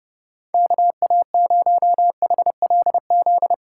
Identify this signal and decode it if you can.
Morse code, heard as a keyed tone, about 37 wpm, 700 Hz. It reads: KA0HLZ
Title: Speed Racing - USA 2x3 Call Signs 30wpm